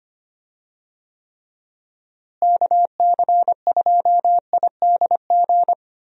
Morse code, heard as a keyed tone, about 25 wpm, 700 Hz. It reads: KC2IDG
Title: USA Call Signs - 6 Characters - Repeated 2x - 25wpm